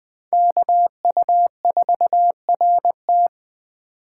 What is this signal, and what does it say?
Morse code, heard as a keyed tone, about 20 wpm, 700 Hz. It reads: KU4RT